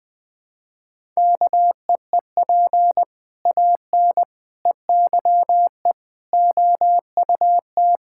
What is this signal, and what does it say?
Morse code, heard as a keyed tone, about 20 wpm, 700 Hz. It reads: KEEP AN EYE OUT
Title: Sets of 6 Words - 20wpm